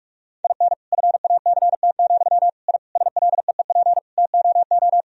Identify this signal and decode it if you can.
Morse code, heard as a keyed tone, about 46 wpm, 700 Hz. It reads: IN FACT, I SLEEP TOO
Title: Sentences from Top 1000 Words - Review Part 3 - 45wpm